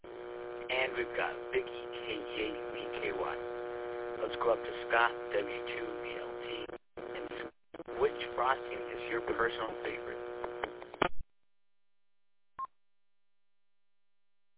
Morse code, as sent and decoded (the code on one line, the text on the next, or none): none